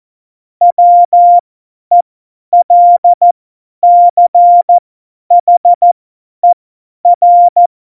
Morse code, heard as a keyed tone, about 14 wpm, spaced as for 10 wpm, 700 Hz. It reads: WELCHER
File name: Morsekurs / Witze / Witz 04. Juni 2019, 50 F70